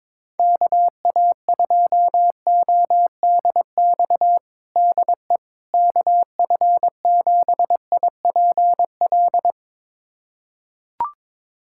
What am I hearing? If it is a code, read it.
Morse code, heard as a keyed tone, about 22 wpm, 700 Hz. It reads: KA2ODX DE KF7IPL
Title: USA Callsign DE Callsign - 6 Characters - 22wpm